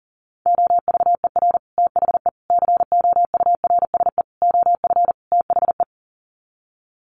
Morse code, heard as a keyed tone, about 40 wpm, 700 Hz. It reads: OVER THE COURSE OF THE
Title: Sets of 5 Words 40wpm